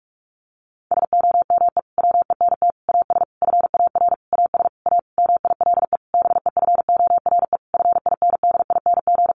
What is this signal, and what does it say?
Morse code, heard as a keyed tone, about 44 wpm, 700 Hz. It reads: SOME WENT AS FAR AS A MILE BEFORE FINDING
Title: Sentences from Top 1100 Words 45wpm